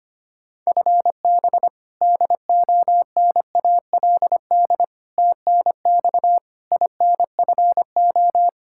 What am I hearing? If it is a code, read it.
Morse code, heard as a keyed tone, about 25 wpm, 700 Hz. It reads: FB DONALD TNX INFO
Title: Practice QSO - Exchange 3 - More Info - 25wpm